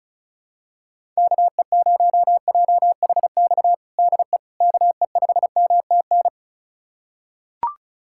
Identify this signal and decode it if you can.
Morse code, heard as a keyed tone, about 36 wpm, 700 Hz. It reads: KE0JHX DE KE5MTN
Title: USA Callsign DE Callsign - 6 Characters - 35wpm